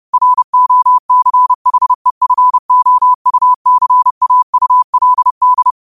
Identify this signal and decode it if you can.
Morse code, heard as a keyed tone, about 30 wpm, 1000 Hz. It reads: ROCHEFOUCAULD